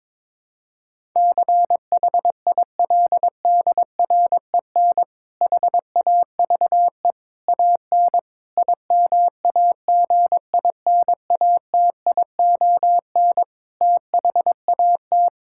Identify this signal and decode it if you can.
Morse code, heard as a keyed tone, about 22 wpm, 700 Hz. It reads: CHILDREN HAVE AN IMAGINATION THAT
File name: Difficult Sentences with Hints - 22wpm